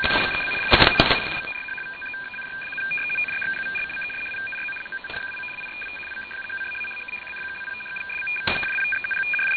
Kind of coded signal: two-tone teleprinter signal